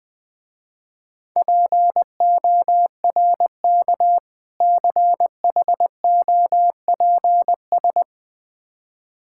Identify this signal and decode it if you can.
Morse code, heard as a keyed tone, about 21 wpm, 700 Hz. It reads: PORK CHOPS
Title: Difficult Words with Hints - 20wpm